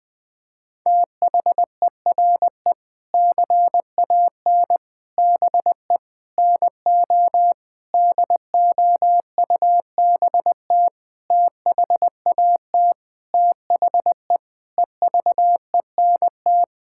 Morse code, heard as a keyed tone, about 19 wpm, 700 Hz. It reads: THERE CAN BE NO DOUBT THAT THE EVENT
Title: Sentences with 7-Word Phrases - 20wpm